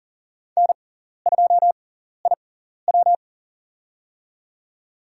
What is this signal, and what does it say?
Morse code, heard as a keyed tone, about 40 wpm, spaced as for 14 wpm, 700 Hz. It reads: N2IW
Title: USA Call Signs - 4 Characters - 6x Character Spacing - 40wpm